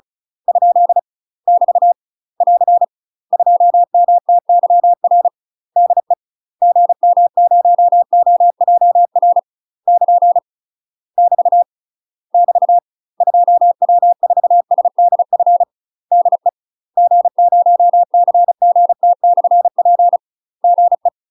Morse code, heard as a keyed tone, about 36 wpm, 700 Hz. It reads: ? <BT> <AR> 2MTYR DE GM0OJR <KN> <BT> <BT> 2W4SDF DE G0CGT/P GE
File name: UK MASTER QSOs TRAINING / 35WPM / UK MASTER QSOs 0005